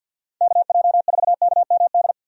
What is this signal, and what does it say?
Morse code, heard as a keyed tone, about 48 wpm, 700 Hz. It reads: KJ4KMD
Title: USA Call Signs - 6 Characters - 50wpm